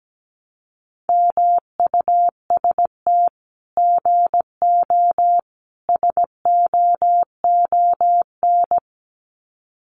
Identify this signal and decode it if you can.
Morse code, heard as a keyed tone, about 17 wpm, 700 Hz. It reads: MUST GO SOON
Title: Common QSO Phrases 17wpm